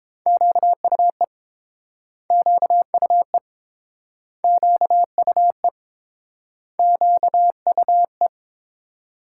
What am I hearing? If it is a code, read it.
Morse code, heard as a keyed tone, about 30 wpm, 700 Hz. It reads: QUE QUE QUE QUE